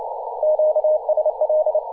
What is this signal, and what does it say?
Morse code, heard as a keyed tone, about 29 wpm, 620 Hz, with radio noise. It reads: QSL
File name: posts / 2017-aug-19-2106z-a35jp-p-14-005mhz-cw